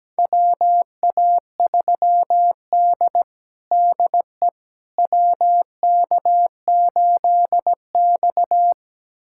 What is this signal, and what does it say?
Morse code, heard as a keyed tone, about 17 wpm, 700 Hz. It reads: WA3D DE WK8X
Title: USA Callsign DE Callsign - 4 Characters - 17wpm